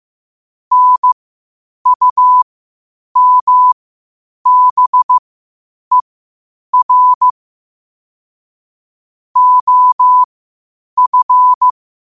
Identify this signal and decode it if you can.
Morse code, heard as a keyed tone, about 15 wpm, spaced as for 8 wpm, 1000 Hz. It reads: NUMBER OF
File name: qotdfiles / 8wpm